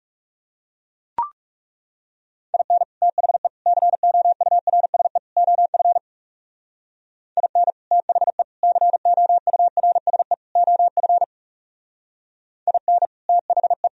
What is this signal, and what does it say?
Morse code, heard as a keyed tone, about 36 wpm, 700 Hz. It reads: IN THE COURSE OF IN THE COURSE OF IN THE
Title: Speed-Racing - Sets of 4 Words 30wpm